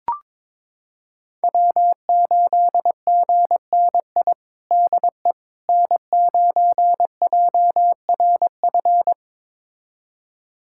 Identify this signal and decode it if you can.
Morse code, heard as a keyed tone, about 22 wpm, 700 Hz. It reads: W8GNI DE N9JRF
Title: USA Callsign DE Callsign - 5 Characters - 22wpm